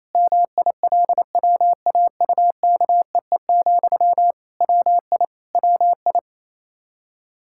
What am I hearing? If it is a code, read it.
Morse code, heard as a keyed tone, about 28 wpm, 700 Hz. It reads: MILWAUKEE, WI WI
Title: Practice QSO - Exchange 1 - Beginning - 28wpm